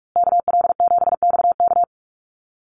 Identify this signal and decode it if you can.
Morse code, heard as a keyed tone, about 44 wpm, 700 Hz. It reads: KL7XK
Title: USA Call Signs Repeated 2x 45wpm